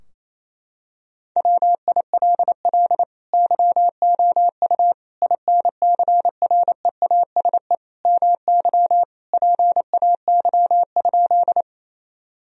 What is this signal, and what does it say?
Morse code, heard as a keyed tone, about 28 wpm, 700 Hz. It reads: WILL YOU INCREASE MY PAY?
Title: Sentences from Top 1000 Words - Review Part 4 - 28wpm